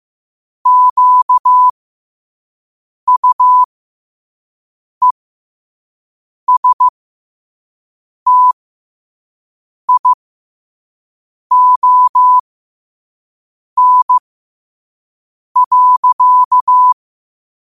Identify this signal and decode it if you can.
Morse code, heard as a keyed tone, about 15 wpm, spaced as for 5 wpm, 1000 Hz. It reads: QUESTION.
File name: qotdfiles / 5wpm